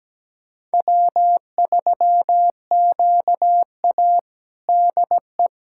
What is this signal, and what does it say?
Morse code, heard as a keyed tone, about 17 wpm, 700 Hz. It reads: W3QA DE
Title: USA Callsign DE Callsign - 4 Characters - 17wpm